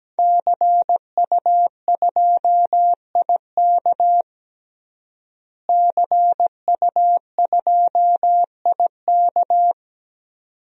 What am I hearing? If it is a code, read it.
Morse code, heard as a keyed tone, about 17 wpm, 700 Hz. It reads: CU2IK CU2IK
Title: DXCC-100 Call Signs - 5 Characters - Repeated 2x - 17wpm